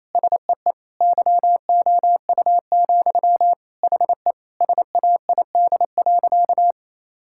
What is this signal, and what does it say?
Morse code, heard as a keyed tone, about 28 wpm, 700 Hz. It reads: SEE YOU, HE SAID.
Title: Sentences from Top 1000 Words - Review Part 6 - 28wpm